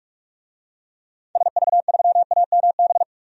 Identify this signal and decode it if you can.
Morse code, heard as a keyed tone, about 41 wpm, 700 Hz. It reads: SV3AMB
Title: DXCC-100 Call Signs - 6 Characters - 45wpm